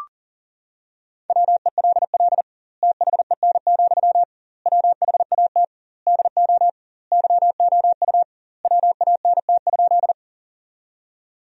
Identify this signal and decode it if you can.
Morse code, heard as a keyed tone, about 40 wpm, 700 Hz. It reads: WELL THEN, WHAT DO YOU WANT?
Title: 2x Word-Spacing - Sentences from Top 200 Words - 40wpm